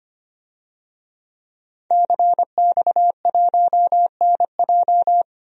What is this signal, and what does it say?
Morse code, heard as a keyed tone, about 25 wpm, 700 Hz. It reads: CX1NJ
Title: DXCC-100 Call Signs - 5 Characters - Repeated 2x - 25wpm